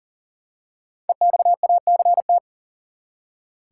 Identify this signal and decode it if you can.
Morse code, heard as a keyed tone, about 40 wpm, 700 Hz. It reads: EXACT
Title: Five Letter Words - 40wpm